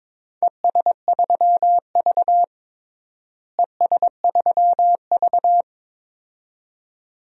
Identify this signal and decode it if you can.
Morse code, heard as a keyed tone, about 22 wpm, 700 Hz. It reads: ES3V ES3V
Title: DXCC-100 Call Signs - 4 Characters - Repeated 2x - 22wpm